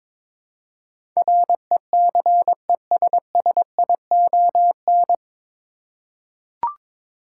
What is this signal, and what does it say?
Morse code, heard as a keyed tone, about 22 wpm, 700 Hz. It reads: RECESSION
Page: Morse Code Ninja practice 22wpm